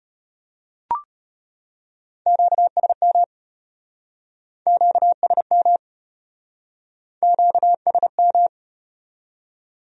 Morse code, heard as a keyed tone, about 33 wpm, 700 Hz. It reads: QSM QSM QSM